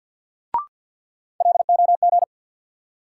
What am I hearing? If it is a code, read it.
Morse code, heard as a keyed tone, about 53 wpm, 700 Hz. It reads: LOG
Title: Sentences from Top 1400 Words - New Word Drill Alt - 50wpm